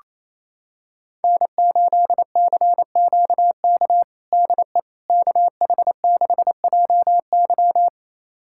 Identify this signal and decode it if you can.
Morse code, heard as a keyed tone, about 28 wpm, 700 Hz. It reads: N8CQK DE KH6JY
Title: USA Callsign DE Callsign - 5 Characters - 28wpm